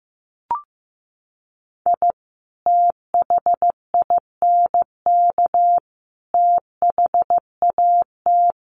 Morse code, heard as a keyed tone, about 15 wpm, 700 Hz. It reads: I THINK THAT
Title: Sets of 5 Words Encore2 15wpm